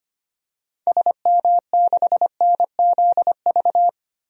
Morse code, heard as a keyed tone, about 25 wpm, 700 Hz. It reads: SM6NZV
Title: DXCC-100 Call Signs - 6 Characters - 25wpm